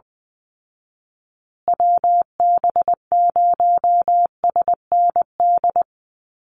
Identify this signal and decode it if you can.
Morse code, heard as a keyed tone, about 20 wpm, 700 Hz. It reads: WB0SND
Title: Call Signs Repeated 3x - 20wpm